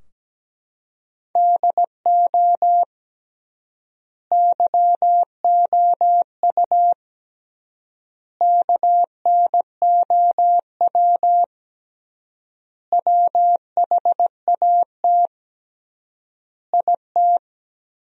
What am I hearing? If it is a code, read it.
Morse code, heard as a keyed tone, about 17 wpm, 700 Hz. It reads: DO YOU KNOW WHAT IT
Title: 3x Word-Spacing - Sentences from Top 100 Words - 17wpm